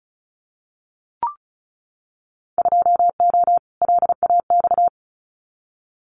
Morse code, heard as a keyed tone, about 34 wpm, 700 Hz. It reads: 2O LAX
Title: Winter Field Day Exchanges 35wpm